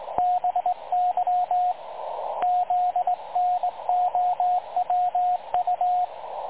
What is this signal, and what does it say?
Morse code, heard as a keyed tone, about 19 wpm, 720 Hz, with radio noise. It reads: BY ZNOWU
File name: dlazaawansowanych / nagraniazpasma / websdr recording 3565.0kHz